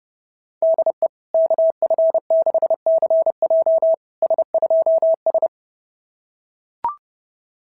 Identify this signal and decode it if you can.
Morse code, heard as a keyed tone, about 31 wpm, 660 Hz. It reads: DE KF6CJ S2S
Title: SOTA QSO Practice 30wpm